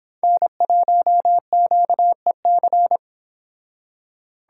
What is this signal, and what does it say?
Morse code, heard as a keyed tone, about 32 wpm, 700 Hz. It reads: N1QEC